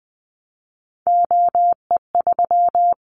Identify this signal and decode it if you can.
Morse code, heard as a keyed tone, about 20 wpm, 700 Hz. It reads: OE3
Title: Callsign Prefixes for Top 100 DXCC - 20wpm